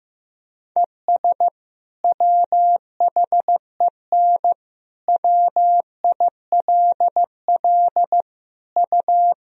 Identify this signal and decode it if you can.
Morse code, heard as a keyed tone, about 15 wpm, 700 Hz. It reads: ES WHEN WILL U
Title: Sentences from Top 100 Words and Abbreviations 15wpm